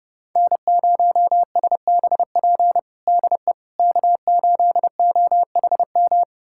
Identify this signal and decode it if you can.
Morse code, heard as a keyed tone, about 30 wpm, 700 Hz. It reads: N0SBP DE K8OHM
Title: USA Callsign DE Callsign - 5 Characters - 30wpm